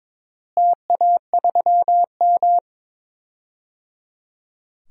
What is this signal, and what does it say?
Morse code, heard as a keyed tone, about 21 wpm, 700 Hz. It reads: TA3M
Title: DXCC-100 Call Signs - 4 Characters - Repeated 3x - 22wpm